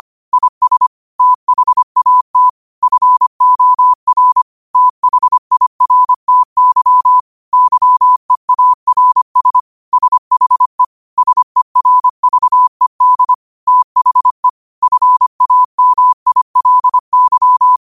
Morse code, heard as a keyed tone, about 25 wpm, 1000 Hz. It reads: IS THAT FOR THIRTY YEARS SHE SERVED THE FAMILY
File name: qotdfiles / 25wpm